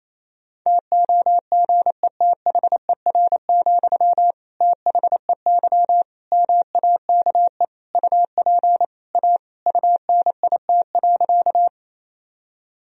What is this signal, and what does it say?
Morse code, heard as a keyed tone, about 28 wpm, 700 Hz. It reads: TOGETHER, THEY MAKE UP A UNIT.
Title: Sentences from Top 1000 Words - Review Part 2 - 28wpm